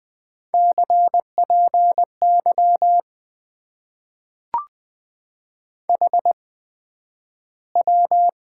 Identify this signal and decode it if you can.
Morse code, heard as a keyed tone, about 21 wpm, spaced as for 11 wpm, 700 Hz. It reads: CPY HW
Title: Top 90 QSO Elements - 8x Character Spacing - 20wpm